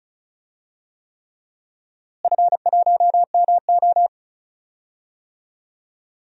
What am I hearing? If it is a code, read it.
Morse code, heard as a keyed tone, about 36 wpm, 700 Hz. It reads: F1MO
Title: DXCC-100 Call Signs - 4 Characters - Repeated 3x - 35wpm